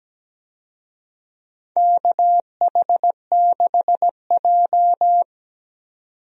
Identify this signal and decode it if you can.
Morse code, heard as a keyed tone, about 17 wpm, 700 Hz. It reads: KH6J
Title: DXCC-100 Call Signs - 4 Characters - 17wpm